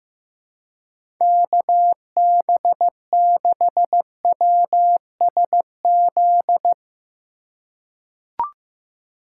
Morse code, heard as a keyed tone, about 16 wpm, 700 Hz. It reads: KB6WSZ
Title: USA Call Signs - 6 Characters - 8x Character Spacing - 15wpm